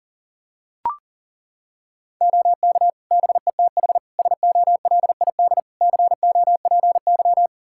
Morse code, heard as a keyed tone, about 40 wpm, 700 Hz. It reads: OK BETH SOLID COPY
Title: Practice QSO - Exchange 3 - More Info - 40wpm